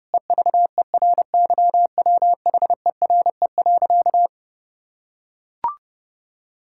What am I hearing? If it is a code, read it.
Morse code, heard as a keyed tone, about 32 wpm, 700 Hz. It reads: EVERYWHERE.
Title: Sentences from Top 1600 Words 30wpm